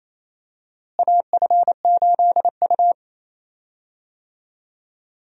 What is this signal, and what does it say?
Morse code, heard as a keyed tone, about 27 wpm, 700 Hz. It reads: AF8U
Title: USA Call Signs - 4 Characters - Repeated 2x - 28wpm